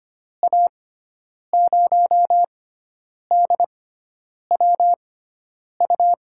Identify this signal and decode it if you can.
Morse code, heard as a keyed tone, about 25 wpm, spaced as for 9 wpm, 700 Hz. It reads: A0DWU
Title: USA Call Signs - 6 Characters - 6x Character Spacing - 25wpm